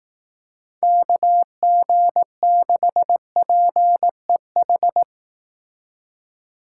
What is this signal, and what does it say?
Morse code, heard as a keyed tone, about 18 wpm, 700 Hz. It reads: KG6PEH
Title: Speed Racing - USA Call Signs 15wpm